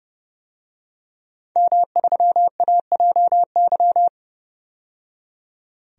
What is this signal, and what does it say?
Morse code, heard as a keyed tone, about 32 wpm, 700 Hz. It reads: M3AJY
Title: DXCC-100 Call Signs - 5 Characters - 30wpm